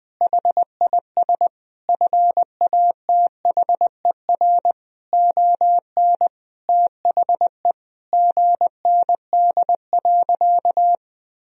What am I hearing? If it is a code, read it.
Morse code, heard as a keyed tone, about 21 wpm, 700 Hz. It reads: HIS FATHER ON THE GND.
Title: Sentences from Top 500 Words and Abbreviations 20wpm